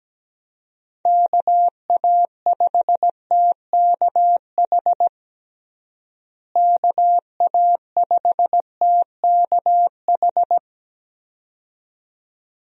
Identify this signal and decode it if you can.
Morse code, heard as a keyed tone, about 17 wpm, 700 Hz. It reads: KA5TKH KA5TKH
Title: USA Call Signs - 6 Characters - Repeated 2x - 17wpm